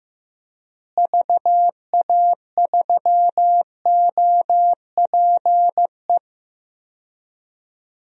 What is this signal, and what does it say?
Morse code, heard as a keyed tone, about 16 wpm, 700 Hz. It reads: VA3OPE
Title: DXCC-100 Call Signs - 6 Characters - Repeated 2x - 15wpm